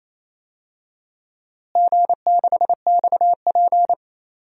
Morse code, heard as a keyed tone, about 28 wpm, 700 Hz. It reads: G6XP
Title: DXCC-100 Call Signs - 4 Characters - Repeated 3x - 28wpm